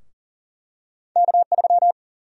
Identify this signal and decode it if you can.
Morse code, heard as a keyed tone, about 39 wpm, 700 Hz. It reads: K3
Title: USA Call Sign Prefixes - Repeated 2x - 40wpm